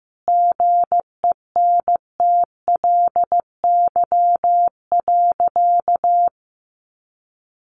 Morse code, heard as a keyed tone, about 15 wpm, 700 Hz. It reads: GENTLY.
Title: Sentences from Top 1500 Words 15wpm